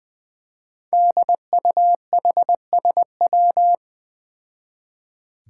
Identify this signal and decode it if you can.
Morse code, heard as a keyed tone, about 20 wpm, 700 Hz. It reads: DUHSW
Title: Five Challenging Letter-Number Combinations 20wpm